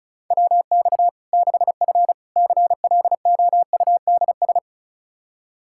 Morse code, heard as a keyed tone, about 34 wpm, 700 Hz. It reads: WX 6F CLOUDS
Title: Practice QSO - Exchange 2 - Info - 35wpm